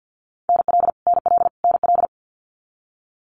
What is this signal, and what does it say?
Morse code, heard as a keyed tone, about 50 wpm, 700 Hz. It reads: NL NL NL
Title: Canadian Provinces and Territories - Repeated 3x - 50wpm